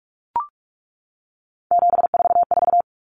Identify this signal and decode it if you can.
Morse code, heard as a keyed tone, about 44 wpm, 700 Hz. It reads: 744
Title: Common Cut Numbers - 3-Digit 45wpm